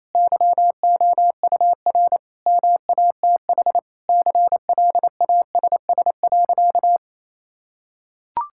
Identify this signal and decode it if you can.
Morse code, heard as a keyed tone, about 28 wpm, 700 Hz. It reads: YOUR MATH CLASS.
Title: Sentences from Top 1000 Words - Review Part 6 - 28wpm